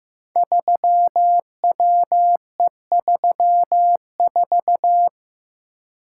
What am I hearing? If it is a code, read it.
Morse code, heard as a keyed tone, about 15 wpm, 700 Hz. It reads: 3WE34
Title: Code Groups - 5 Letter:Numbers - 15wpm